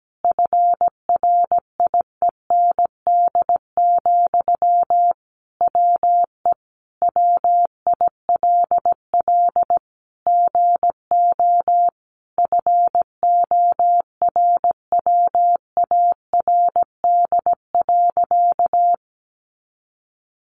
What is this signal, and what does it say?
Morse code, heard as a keyed tone, about 17 wpm, 700 Hz. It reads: FRIEND, WE WILL GO FORWARD.
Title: Sentences from Top 1300 Words 17wpm